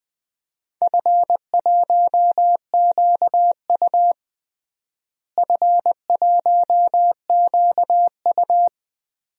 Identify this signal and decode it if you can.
Morse code, heard as a keyed tone, about 21 wpm, 700 Hz. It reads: F1QU F1QU
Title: DXCC-100 Call Signs - 4 Characters - Repeated 2x - 20wpm